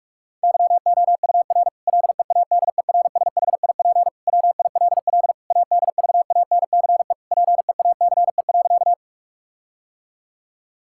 Morse code, heard as a keyed tone, about 46 wpm, 700 Hz. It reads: YOUR LEADERSHIP WILL ADVANCE PEACE.
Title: Sentences from Top 1600 Words 45wpm